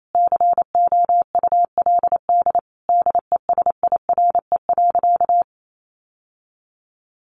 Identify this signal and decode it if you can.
Morse code, heard as a keyed tone, about 28 wpm, 700 Hz. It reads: COULD DESIRE.
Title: Sentences from Top 1100 Words 28wpm